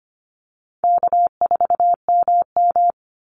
Morse code, heard as a keyed tone, about 25 wpm, 700 Hz. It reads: K4MM
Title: Easy Call Signs Repeated 2x - 25wpm